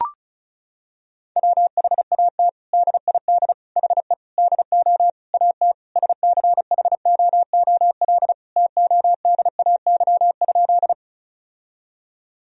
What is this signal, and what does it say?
Morse code, heard as a keyed tone, about 34 wpm, 700 Hz. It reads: WHAT DID HE DO AT SCHOOL TODAY?
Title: Sentences from Top 1000 Words - Review Part 1 - 35wpm